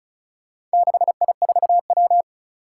Morse code, heard as a keyed tone, about 36 wpm, 700 Hz. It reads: 6I4W
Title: Four Challenging Letter-Number Combinations 35wpm